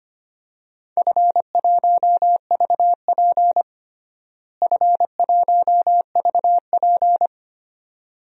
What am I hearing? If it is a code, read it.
Morse code, heard as a keyed tone, about 25 wpm, 700 Hz. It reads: F1VP F1VP
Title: DXCC-100 Call Signs - 4 Characters - Repeated 2x - 25wpm